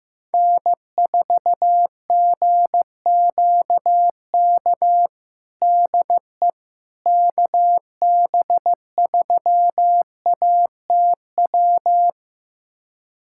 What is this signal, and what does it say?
Morse code, heard as a keyed tone, about 15 wpm, 700 Hz. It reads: N4GQK DE KB3ATW
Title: USA Callsign DE Callsign 15wpm